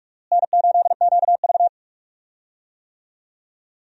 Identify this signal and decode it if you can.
Morse code, heard as a keyed tone, about 46 wpm, 700 Hz. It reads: N8QV
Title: USA Call Signs - 4 Characters - Repeated 2x - 45wpm